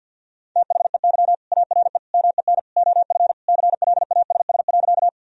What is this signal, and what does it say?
Morse code, heard as a keyed tone, about 50 wpm, 700 Hz. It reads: THEY ARE MEN OF CLASS.
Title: Sentences from Top 1000 Words - Review Part 2 - 50wpm